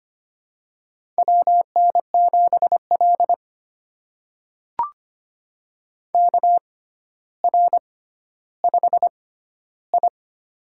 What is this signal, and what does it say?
Morse code, heard as a keyed tone, about 25 wpm, spaced as for 13 wpm, 700 Hz. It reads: WN7L KR5I
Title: USA Call Signs - 4 Characters - 6x Character Spacing - 25wpm